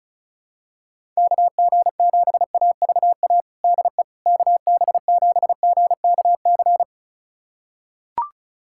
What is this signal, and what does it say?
Morse code, heard as a keyed tone, about 36 wpm, 700 Hz. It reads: KG7AVA DE KB7GKC
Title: USA Callsign DE Callsign - 6 Characters - 35wpm